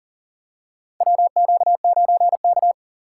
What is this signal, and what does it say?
Morse code, heard as a keyed tone, about 40 wpm, 700 Hz. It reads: WQ9K
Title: USA Call Signs - 4 Characters - 4x Character Spacing - 40wpm